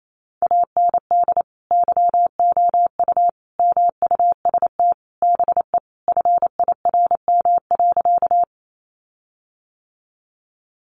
Morse code, heard as a keyed tone, about 28 wpm, 700 Hz. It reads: AND YOU MUST BE FIRM.
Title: Sentences from Top 1400 Words 28wpm